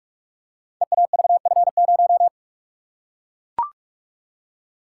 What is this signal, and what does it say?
Morse code, heard as a keyed tone, about 44 wpm, 690 Hz. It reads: EAVF0
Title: Code Groups - 5 Letter:Numbers - 45wpm